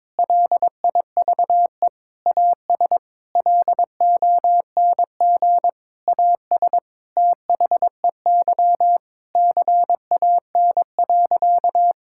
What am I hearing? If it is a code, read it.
Morse code, heard as a keyed tone, about 22 wpm, 700 Hz. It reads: LIVE AS LONG AS THEY CAN.
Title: Sentences from Top 1000 Words - Review Part 5 - 22wpm